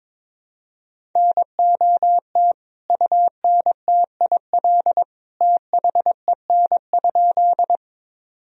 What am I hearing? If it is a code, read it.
Morse code, heard as a keyed tone, about 22 wpm, 700 Hz. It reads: NOT UNTIL THEN?
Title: Sentences from Top 1000 Words - Review Part 1 - 22wpm